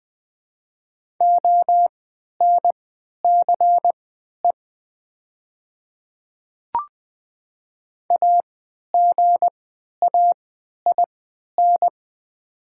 Morse code, heard as a keyed tone, about 20 wpm, spaced as for 11 wpm, 700 Hz. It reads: ONCE AGAIN
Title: 3x Character Spacing - Top 300 Words - 20wpm